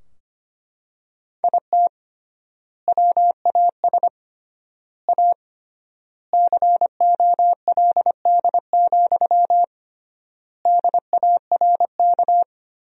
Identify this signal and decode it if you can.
Morse code, heard as a keyed tone, about 25 wpm, 700 Hz. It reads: IT WAS A COLD, DARK
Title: 3x Word-Spacing - Sentences from Top 500 Words - 25wpm